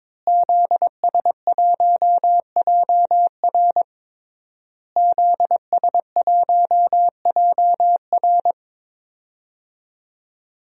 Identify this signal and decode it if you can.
Morse code, heard as a keyed tone, about 22 wpm, 700 Hz. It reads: ZS1JR ZS1JR
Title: DXCC-100 Call Signs - 5 Characters - Repeated 3x - 22wpm